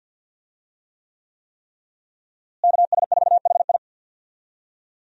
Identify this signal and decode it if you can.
Morse code, heard as a keyed tone, about 52 wpm, 700 Hz. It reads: KS4HI
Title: USA Call Signs - 5 Characters - Repeated 3x - 50wpm